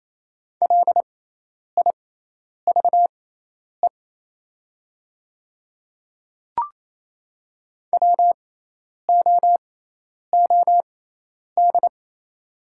Morse code, heard as a keyed tone, about 28 wpm, spaced as for 10 wpm, 700 Hz. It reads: LIVE WOOD
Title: 6x Character Spacing - Top 300 Words - 28wpm